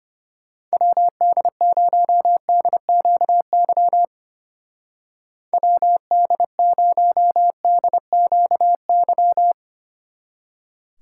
Morse code, heard as a keyed tone, about 31 wpm, 700 Hz. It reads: WD0DQY WD0DQY
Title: Speed Racing - USA 2x3 Call Signs 25wpm